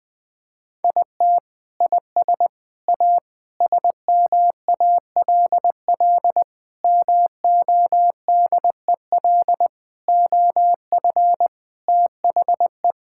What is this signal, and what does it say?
Morse code, heard as a keyed tone, about 20 wpm, 700 Hz. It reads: IT IS A SMALL MODEL OF THE
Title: Sentences from Top 1000 Words - Review Part 5 - 20wpm